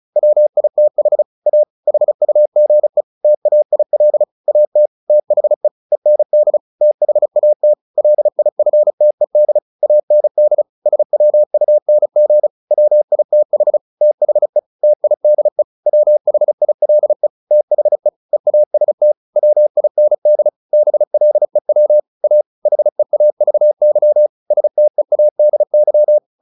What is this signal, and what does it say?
Morse code, heard as a keyed tone, about 34 wpm, 600 Hz. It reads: WITH A HUGE TAIL AT THE END THAT LIFTED AND SWUNG WITH THE TIDE WHILE THE EAST WIND BLEW A HEAVY STEADY
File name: the old man and the sea 2 35wpm